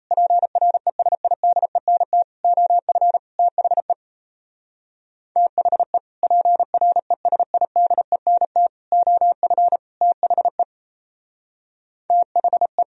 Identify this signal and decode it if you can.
Morse code, heard as a keyed tone, about 35 wpm, 700 Hz. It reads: PRESIDENT OF THE THE PRESIDENT OF THE THE
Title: Speed-Racing - Sets of 4 Words 28wpm